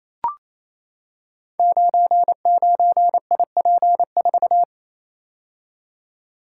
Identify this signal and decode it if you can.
Morse code, heard as a keyed tone, about 28 wpm, 700 Hz. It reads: 99IP4
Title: Code Groups - 5 Letter:Numbers - 28wpm